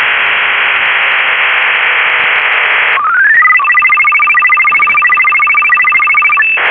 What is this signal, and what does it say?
SSTV (slow-scan television), Robot 36.